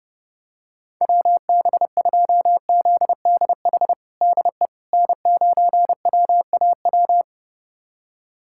Morse code, heard as a keyed tone, about 32 wpm, 700 Hz. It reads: WB2ZDH DE N9WAW
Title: USA Callsign DE Callsign 30wpm